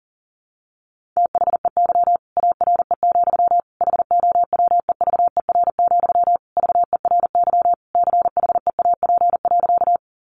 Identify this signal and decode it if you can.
Morse code, heard as a keyed tone, about 37 wpm, 700 Hz. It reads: THEY ARE, HOWEVER, VERY CHEAP.
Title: Sentences from Top 1200 Words 40wpm